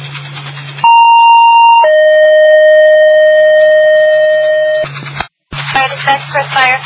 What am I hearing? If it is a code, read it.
Two-tone page, heard as a one-second tone, then a three-second tone, about 930 then 620 Hz, with speech. It is heard from 0.8 to 4.8 s.